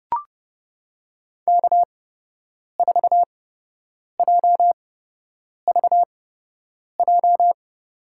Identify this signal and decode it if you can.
Morse code, heard as a keyed tone, about 31 wpm, spaced as for 8 wpm, 700 Hz. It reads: K4JVJ
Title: USA Call Signs - 8x Character Spacing - 30wpm